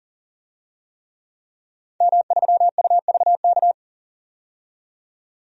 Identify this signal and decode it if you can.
Morse code, heard as a keyed tone, about 40 wpm, 700 Hz. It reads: M3UVK
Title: DXCC-100 Call Signs - 5 Characters - 40wpm